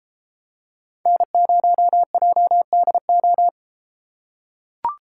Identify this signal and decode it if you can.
Morse code, heard as a keyed tone, about 33 wpm, 700 Hz. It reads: N0JDO